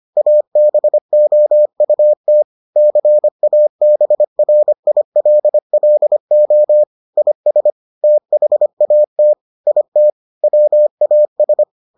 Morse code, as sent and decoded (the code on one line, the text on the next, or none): .- -... --- ..- - / -.-. .- -... .-. .. .-.. .-.. --- / .. ... / - .... .- - / .. - / .-- .- ...
ABOUT CABRILLO IS THAT IT WAS